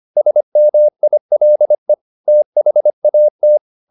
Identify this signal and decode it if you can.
Morse code, heard as a keyed tone, about 25 wpm, 600 Hz. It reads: SMILE THAT